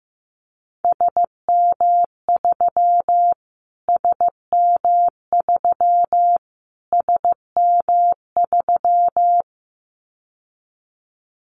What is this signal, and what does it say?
Morse code, heard as a keyed tone, about 15 wpm, 700 Hz. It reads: SM3 SM3 SM3
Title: Callsign Prefixes for Top 25 DXCC - Repeated 3x - 15wpm